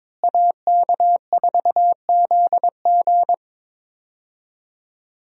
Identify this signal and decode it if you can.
Morse code, heard as a keyed tone, about 21 wpm, 700 Hz. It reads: AK4ZG
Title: USA Call Signs - 5 Characters - Repeated 2x - 22wpm